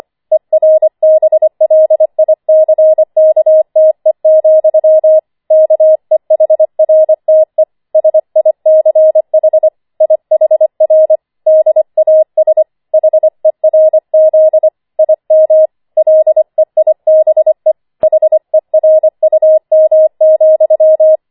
Morse code, heard as a keyed tone, about 25 wpm, 630 Hz. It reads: ERBLICKTE, KEHRTE SICH IHR DAS HERZ IM LEIBE HERUM,
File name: 25wpm